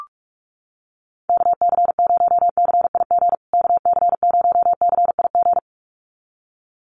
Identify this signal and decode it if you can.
Morse code, heard as a keyed tone, about 44 wpm, 700 Hz. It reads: KC0CIG KC0CIG
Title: USA Call Signs Repeated 2x 45wpm